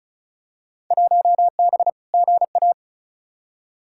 Morse code, heard as a keyed tone, about 36 wpm, 700 Hz. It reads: 1B GA
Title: Field Day Exchanges - 35wpm